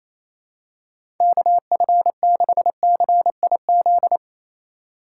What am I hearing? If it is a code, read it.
Morse code, heard as a keyed tone, about 28 wpm, 700 Hz. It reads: KF6CIZ
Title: USA Call Signs - 6 Characters - 2x Character Spacing - 28wpm